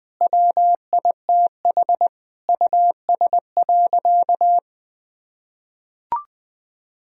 Morse code, heard as a keyed tone, about 20 wpm, 700 Hz. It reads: WITH US.
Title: Sentences from Top 1000 Words - Review Part 2 - 20wpm